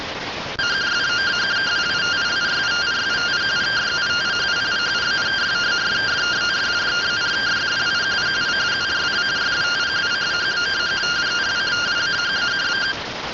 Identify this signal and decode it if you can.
Two-tone teleprinter signal, about 170 Hz shift.